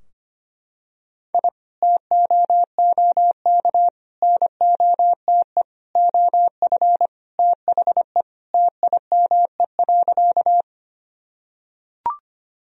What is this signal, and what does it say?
Morse code, heard as a keyed tone, about 25 wpm, 700 Hz. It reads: I TOOK NOTE OF THE TIME.
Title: Sentences from Top 1000 Words - Review Part 2 - 25wpm